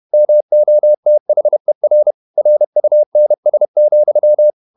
Morse code, heard as a keyed tone, about 31 wpm, 600 Hz. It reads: MOTHER RUNS,
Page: Never 31wpm | CW med Gnister